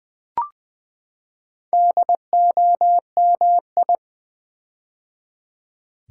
Morse code, heard as a keyed tone, about 20 wpm, 700 Hz. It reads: DOMI
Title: Build A Word - 4 Letters At A Time - 8-Letter Words - 20wpm